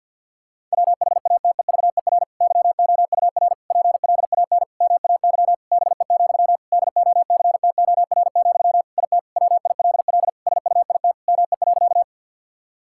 Morse code, heard as a keyed tone, about 50 wpm, 700 Hz. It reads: WHATEVER YOUR PLAN MAY BE, DOCTOR, IT WILL SUIT ME.
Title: Sentences from Top 1600 Words 50wpm